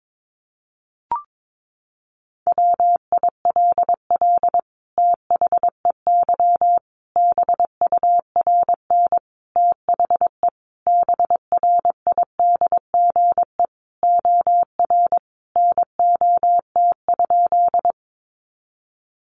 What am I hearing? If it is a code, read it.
Morse code, heard as a keyed tone, about 22 wpm, 700 Hz. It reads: WILL THEY BURN THE BRIDGE OR NOT?
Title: Sentences from Top 1300 Words 22wpm